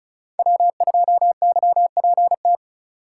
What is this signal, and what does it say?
Morse code, heard as a keyed tone, about 36 wpm, 700 Hz. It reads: W2YPT
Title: USA Call Signs - 5 Characters - 2x Character Spacing - 35wpm